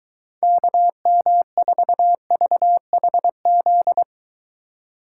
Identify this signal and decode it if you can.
Morse code, heard as a keyed tone, about 23 wpm, 700 Hz. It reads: KM4VHZ